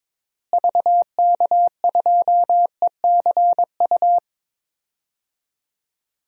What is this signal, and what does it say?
Morse code, heard as a keyed tone, about 22 wpm, 700 Hz. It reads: VK2ECU
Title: DXCC-100 Call Signs - 6 Characters - Repeated 3x - 22wpm